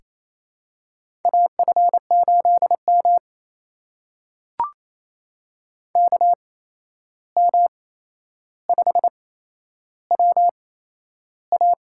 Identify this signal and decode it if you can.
Morse code, heard as a keyed tone, about 28 wpm, spaced as for 11 wpm, 700 Hz. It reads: AF8M KM5WA
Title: USA Call Signs - 8x Character Spacing - 28wpm